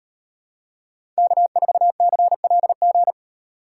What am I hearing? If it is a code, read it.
Morse code, heard as a keyed tone, about 33 wpm, 700 Hz. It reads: K4CLG